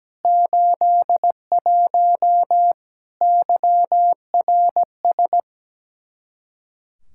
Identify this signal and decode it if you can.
Morse code, heard as a keyed tone, about 17 wpm, 700 Hz. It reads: 81 YRS
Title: Practice QSO - Exchange 3 - More Info - 17wpm